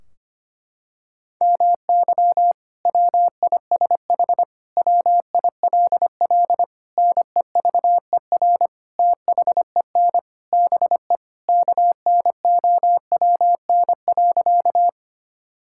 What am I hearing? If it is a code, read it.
Morse code, heard as a keyed tone, about 25 wpm, 700 Hz. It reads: MY WISH WILL NEVER THEN BE KNOWN.
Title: Sentences from Top 1600 Words 25wpm